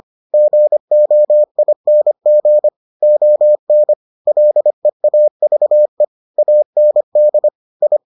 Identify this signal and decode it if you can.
Morse code, heard as a keyed tone, about 25 wpm, 600 Hz. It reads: GOING ON LEAVE AND I